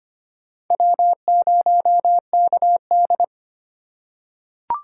Morse code, heard as a keyed tone, about 25 wpm, 700 Hz. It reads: W0KD
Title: USA Call Signs - 4 Characters - 2x Character Spacing - 25wpm